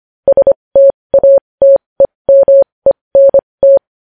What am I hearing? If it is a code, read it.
Morse code, heard as a keyed tone, about 25 wpm, spaced as for 20 wpm, 550 Hz. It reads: STATEMENT
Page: News Headlines in Morse Code at 20 WPM